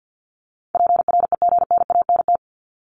Morse code, heard as a keyed tone, about 50 wpm, 700 Hz. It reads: PREGNANT
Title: Top 751-1000 Words 50wpm